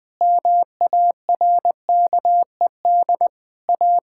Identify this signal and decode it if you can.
Morse code, heard as a keyed tone, about 20 wpm, 700 Hz. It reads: MARKED A
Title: Difficult Sentences with Hints - 20wpm